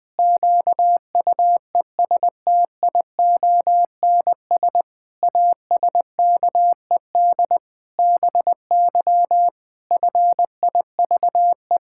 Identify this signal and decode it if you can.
Morse code, heard as a keyed tone, about 19 wpm, 700 Hz. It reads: QUESTIONS ASKED BY FIVE
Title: Pangram Challenge v2 - 20wpm